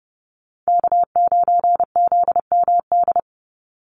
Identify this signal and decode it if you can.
Morse code, heard as a keyed tone, about 30 wpm, 700 Hz. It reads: K9ZMD
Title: NA SOTA Chaser Callsigns 30wpm